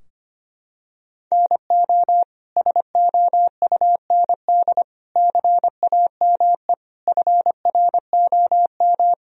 Morse code, heard as a keyed tone, about 25 wpm, 700 Hz. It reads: NO SOUND CAME FROM
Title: Sentences from Top 1000 Words - Review Part 4 - 25wpm